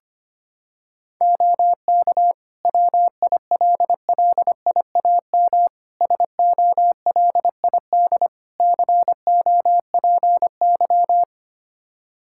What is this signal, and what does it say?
Morse code, heard as a keyed tone, about 25 wpm, 700 Hz. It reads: OK WILLIAM SOLID COPY
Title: Practice QSO - Exchange 3 - More Info - 25wpm